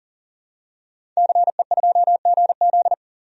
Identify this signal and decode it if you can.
Morse code, heard as a keyed tone, about 40 wpm, 700 Hz. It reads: CE2GZ
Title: DXCC-100 Call Signs - 5 Characters - 40wpm